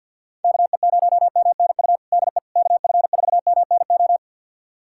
Morse code, heard as a keyed tone, about 52 wpm, 700 Hz. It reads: KE0MNU DE KF4MNO
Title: USA Callsign DE Callsign - 6 Characters - 50wpm